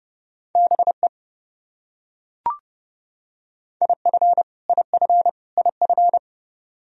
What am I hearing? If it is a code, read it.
Morse code, heard as a keyed tone, about 32 wpm, 700 Hz. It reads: BE IF IF IF
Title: Top 100 Words Repeated 30wpm